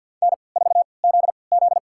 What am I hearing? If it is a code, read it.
Morse code, heard as a keyed tone, about 53 wpm, spaced as for 29 wpm, 700 Hz. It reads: N4ZZ
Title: USA Call Signs - 4 Characters - 3x Character Spacing - 50wpm